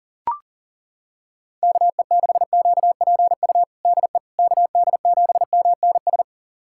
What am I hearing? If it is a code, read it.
Morse code, heard as a keyed tone, about 40 wpm, 700 Hz. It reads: KE6QPU DE KD7MNS
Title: USA Callsign DE Callsign 40wpm